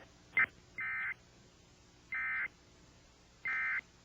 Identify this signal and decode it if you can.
Emergency Alert System (SAME) tones, heard as harsh buzzing bursts. It is heard from 0.8 to 3.8 s.